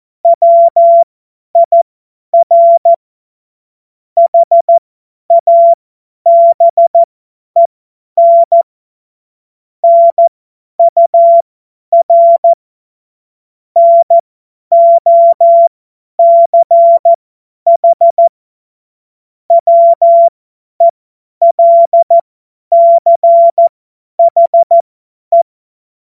Morse code, heard as a keyed tone, about 14 wpm, spaced as for 10 wpm, 680 Hz. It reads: WIR HABEN NUR NOCH WELCHE
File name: Morsekurs / Witze / Witz 17.Oktober 2023 50 F70